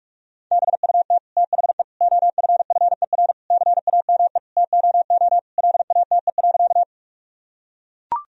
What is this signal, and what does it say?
Morse code, heard as a keyed tone, about 41 wpm, 700 Hz. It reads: BUT THE OFFER CAME TOO LATE.
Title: Sentences from Top 1000 Words - Review Part 4 - 45wpm